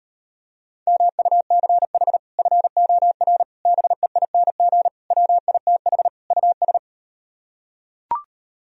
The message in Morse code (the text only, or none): MUCH FOR BEING WITH US